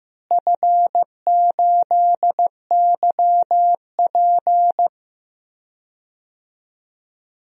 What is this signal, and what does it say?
Morse code, heard as a keyed tone, about 15 wpm, 700 Hz. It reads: F8YP
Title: DXCC-100 Call Signs - 4 Characters - Repeated 2x - 15wpm